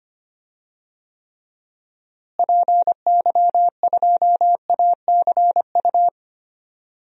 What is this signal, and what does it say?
Morse code, heard as a keyed tone, about 25 wpm, 700 Hz. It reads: PY2ACU
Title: DXCC-100 Call Signs - 6 Characters - Repeated 3x - 25wpm